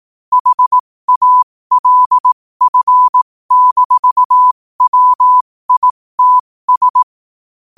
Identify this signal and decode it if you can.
Morse code, heard as a keyed tone, about 18 wpm, spaced as for 15 wpm, 1000 Hz. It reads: HALF-WITS
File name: 15wpm